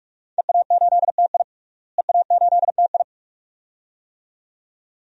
Morse code, heard as a keyed tone, about 43 wpm, 700 Hz. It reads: EA8TI EA8TI
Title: DXCC-100 Call Signs - 5 Characters - Repeated 2x - 45wpm